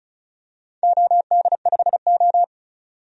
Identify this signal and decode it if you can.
Morse code, heard as a keyed tone, about 34 wpm, 700 Hz. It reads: OD5O
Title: DXCC-100 Call Signs - 4 Characters - 35wpm